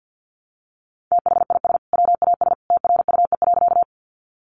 Morse code, heard as a keyed tone, about 50 wpm, 700 Hz. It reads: THIS WAS TRUE.
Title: Sentences from Top 400 Words 50wpm